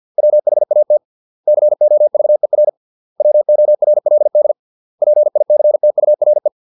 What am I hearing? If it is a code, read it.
Morse code, heard as a keyed tone, about 50 wpm, 600 Hz. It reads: WHAT COVER WORLD PICTURE